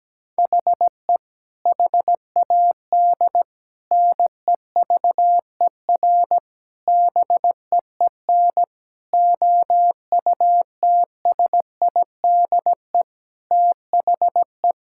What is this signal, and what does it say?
Morse code, heard as a keyed tone, about 17 wpm, 700 Hz. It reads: HE HAD NEVER BEEN OUTSIDE THE
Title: Sentences from Top 1600 Words 17wpm